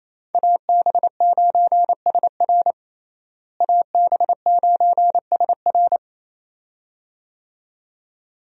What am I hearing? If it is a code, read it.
Morse code, heard as a keyed tone, about 28 wpm, 700 Hz. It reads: AB9SR AB9SR
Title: USA Call Signs - 5 Characters - Repeated 2x - 28wpm